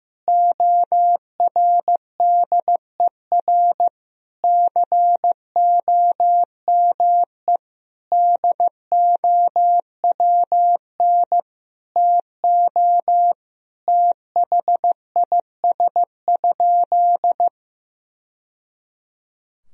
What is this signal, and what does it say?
Morse code, heard as a keyed tone, about 16 wpm, 700 Hz. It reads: ORDER COME DOWN TO THIS?
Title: Sentences from Top 1000 Words - Review Part 3 - 15wpm